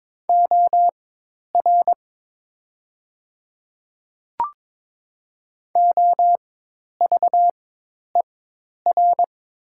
Morse code, heard as a keyed tone, about 22 wpm, spaced as for 10 wpm, 700 Hz. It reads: OR OVER
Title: 4x Character Spacing - Top 200 Words - 22wpm